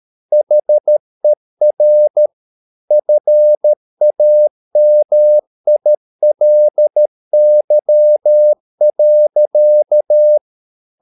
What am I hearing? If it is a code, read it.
Morse code, heard as a keyed tone, about 13 wpm, 600 Hz. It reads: HER FAMILY.